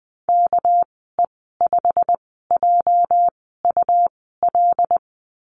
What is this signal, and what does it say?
Morse code, heard as a keyed tone, about 20 wpm, spaced as for 14 wpm, 700 Hz. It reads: KE5JUL
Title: 245 Lesson 39g - Letter z - Call Signs 20wpm